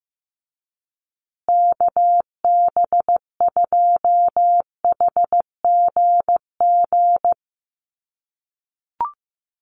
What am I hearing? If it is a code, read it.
Morse code, heard as a keyed tone, about 15 wpm, 700 Hz. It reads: KB2HGG
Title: USA Call Signs Repeated 2x 15wpm